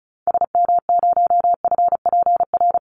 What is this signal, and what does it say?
Morse code, heard as a keyed tone, about 34 wpm, 700 Hz. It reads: SM0FPR
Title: Call Signs Repeated 3x - 35wpm